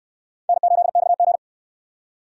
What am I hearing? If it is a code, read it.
Morse code, heard as a keyed tone, about 60 wpm, 700 Hz. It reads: N8XG